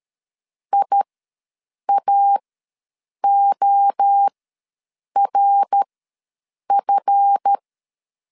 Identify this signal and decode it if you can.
Morse code, heard as a keyed tone, about 13 wpm, spaced as for 7 wpm, 800 Hz. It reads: IAORF
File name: LEZIONI / 8 YJ45 / 8 YJ45 ws 12wpm 3146